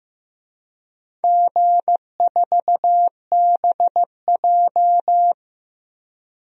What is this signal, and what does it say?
Morse code, heard as a keyed tone, about 15 wpm, 700 Hz. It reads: G4BJ
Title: DXCC-100 Call Signs - 4 Characters - Repeated 2x - 15wpm